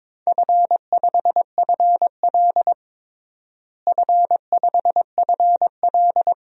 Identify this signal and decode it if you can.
Morse code, heard as a keyed tone, about 22 wpm, 700 Hz. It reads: F5FL F5FL
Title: DXCC-100 Call Signs - 4 Characters - Repeated 2x - 22wpm